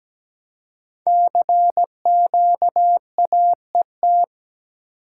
Morse code, heard as a keyed tone, about 17 wpm, 700 Hz. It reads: CQAET
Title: Code Groups - 5 Letter:Numbers - 17wpm